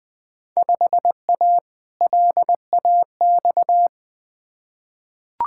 Morse code, heard as a keyed tone, about 20 wpm, 700 Hz. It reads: 5A LAX
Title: Field Day Exchanges - 20wpm